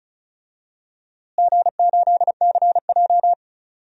Morse code, heard as a keyed tone, about 34 wpm, 700 Hz. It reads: G8CJ
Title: DXCC-100 Call Signs - 4 Characters - Repeated 3x - 35wpm